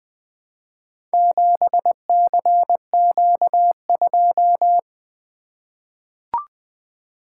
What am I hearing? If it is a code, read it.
Morse code, heard as a keyed tone, about 20 wpm, 700 Hz. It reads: 7CQ2
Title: Four Letter-Number Combinations - 20wpm